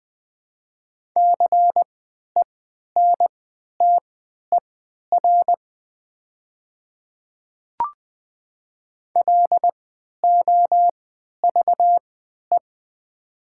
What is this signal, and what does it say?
Morse code, heard as a keyed tone, about 20 wpm, spaced as for 11 wpm, 700 Hz. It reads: CENTER LOVE
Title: 3x Character Spacing - Top 500 Words - 20wpm